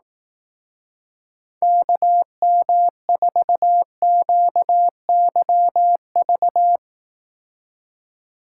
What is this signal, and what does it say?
Morse code, heard as a keyed tone, about 18 wpm, 700 Hz. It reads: KM4QYV
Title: Speed Racing - USA 2x3 Call Signs 15wpm